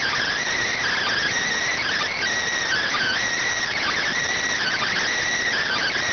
Emergency Alert System (SAME) tones (header bursts), heard as harsh buzzing bursts. It is heard from 0.7 to 5.1 s.